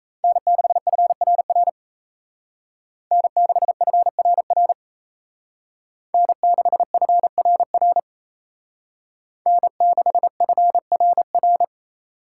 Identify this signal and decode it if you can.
Morse code, heard as a keyed tone, about 33 wpm, 700 Hz. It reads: N6FRR N6FRR N6FRR N6FRR
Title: Speed Racing - USA 1x3 Call Signs 28wpm